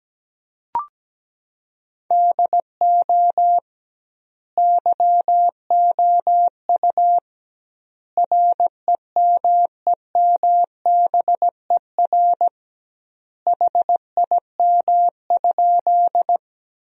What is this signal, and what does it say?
Morse code, heard as a keyed tone, about 17 wpm, 700 Hz. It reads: DO YOU REMEMBER HIM?
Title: 2x Word-Spacing - Sentences from Top 400 Words - 17wpm